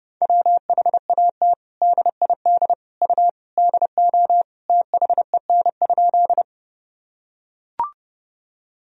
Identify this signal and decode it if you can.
Morse code, heard as a keyed tone, about 30 wpm, 700 Hz. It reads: WHAT DID U DO THEN?
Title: Sentences from Top 100 Words and Abbreviations 30wpm